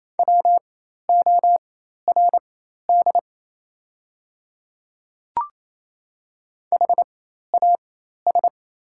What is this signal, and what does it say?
Morse code, heard as a keyed tone, about 28 wpm, spaced as for 13 wpm, 700 Hz. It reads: WORD HAS
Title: 4x Character Spacing - Top 100 Words - 28wpm